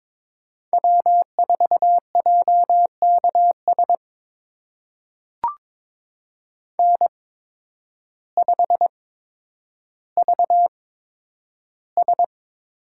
Morse code, heard as a keyed tone, about 22 wpm, spaced as for 10 wpm, 700 Hz. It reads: W4JKS N5VS
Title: USA Call Signs - 5 Characters - 8x Character Spacing - 22wpm